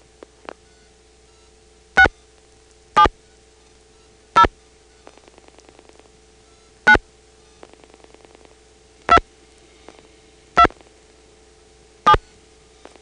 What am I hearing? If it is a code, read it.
DTMF, heard as telephone touch tones, tones of about 80 ms. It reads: 6 7 0 9 2 2 7
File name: phonetrips / PrinceWmCoVA5HQ